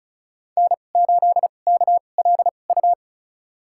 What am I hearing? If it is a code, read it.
Morse code, heard as a keyed tone, about 34 wpm, spaced as for 25 wpm, 700 Hz. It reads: N8KLU
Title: USA Call Signs - 2x Character Spacing - 35wpm